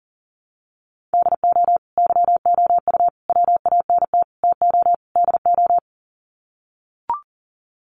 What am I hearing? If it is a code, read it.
Morse code, heard as a keyed tone, about 40 wpm, 700 Hz. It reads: DO YOU WANT TO DO
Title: Sets of 5 Words Encore 40wpm